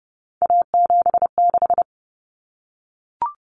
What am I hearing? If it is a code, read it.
Morse code, heard as a keyed tone, about 30 wpm, 700 Hz. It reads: A76
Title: Common Cut Numbers - 3-Digit 30wpm